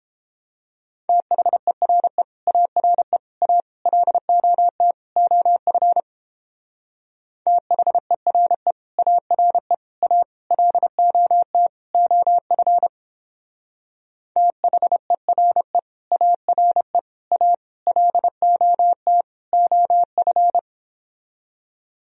Morse code, heard as a keyed tone, about 29 wpm, 700 Hz. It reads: THERE ARE A LOT OF THERE ARE A LOT OF THERE ARE A LOT OF